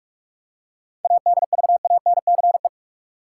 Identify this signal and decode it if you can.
Morse code, heard as a keyed tone, about 44 wpm, 700 Hz. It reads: ADVANCE
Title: Seven Letter Words - 45wpm